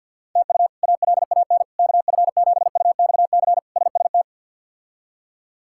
Morse code, heard as a keyed tone, about 52 wpm, 700 Hz. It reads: TU ALAN KF7UXC SST
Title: K1USN SST QSO Practice - 50wpm